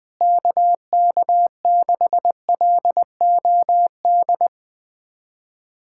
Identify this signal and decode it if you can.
Morse code, heard as a keyed tone, about 21 wpm, 700 Hz. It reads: KK6LOD
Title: USA Callsign DE Callsign - 6 Characters - 20wpm